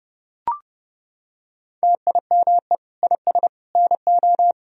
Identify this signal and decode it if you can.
Morse code, heard as a keyed tone, about 32 wpm, 700 Hz. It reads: TIME IS NO
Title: Sentences from Top 1000 Words - Review Part 3 - 30wpm